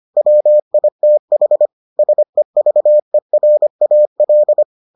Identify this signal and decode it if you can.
Morse code, heard as a keyed tone, about 25 wpm, 600 Hz. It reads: WITH SEVERAL